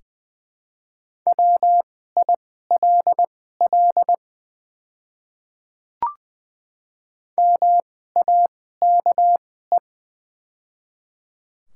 Morse code, heard as a keyed tone, about 20 wpm, spaced as for 14 wpm, 700 Hz. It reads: WILL MAKE
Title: 2x Character Spacing - Top 200 Words - 20wpm